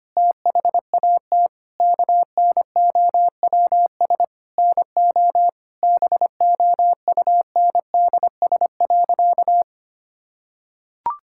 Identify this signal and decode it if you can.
Morse code, heard as a keyed tone, about 25 wpm, 700 Hz. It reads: THAT KNOWS NO BOUNDS.
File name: practice / 25wpm / Difficult Sentences with Hints - 25wpm